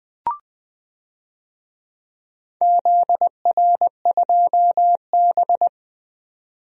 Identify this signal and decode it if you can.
Morse code, heard as a keyed tone, about 20 wpm, 700 Hz. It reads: ZR2B
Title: DXCC-100 Call Signs - 4 Characters - 20wpm